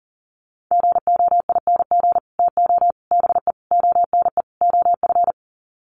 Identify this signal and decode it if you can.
Morse code, heard as a keyed tone, about 40 wpm, 700 Hz. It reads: GOING TO BE ONE OF
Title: Sets of 5 Words Encore2 40wpm